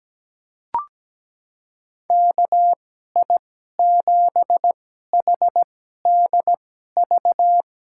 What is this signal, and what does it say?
Morse code, heard as a keyed tone, about 17 wpm, spaced as for 12 wpm, 700 Hz. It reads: KI7HDV
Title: USA Call Signs - 6 Characters - 2x Character Spacing - 17wpm